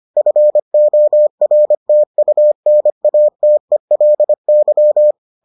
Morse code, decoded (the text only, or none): FORTUNATELY